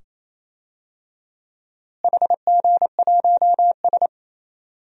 Morse code, heard as a keyed tone, about 28 wpm, 700 Hz. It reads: HG1S
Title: DXCC-100 Call Signs - 4 Characters - Repeated 2x - 28wpm